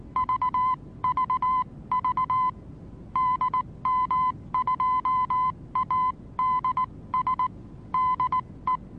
Morse code, heard as a keyed tone, about 19 wpm, spaced as for 16 wpm, 1020 Hz. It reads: VVV DM2ADS DE
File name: repos / Morsekurs / DJ4UF Morsen26